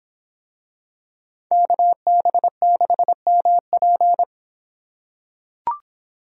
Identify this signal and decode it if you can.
Morse code, heard as a keyed tone, about 25 wpm, 700 Hz. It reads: KB6MP